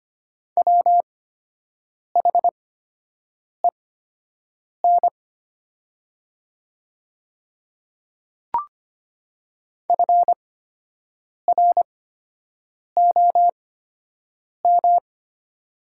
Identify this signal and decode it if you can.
Morse code, heard as a keyed tone, about 25 wpm, spaced as for 7 wpm, 700 Hz. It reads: WHEN FROM
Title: 8x Character Spacing - Top 100 Words - 25wpm